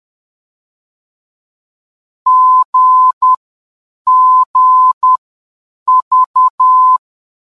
Morse code, heard as a keyed tone, about 10 wpm, spaced as for 7 wpm, 1020 Hz. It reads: GGV
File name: GGV-VOR DME